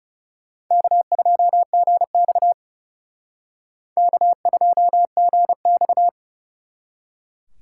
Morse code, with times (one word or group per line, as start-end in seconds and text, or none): 0.7-2.5 K2GX
4.0-6.1 K2GX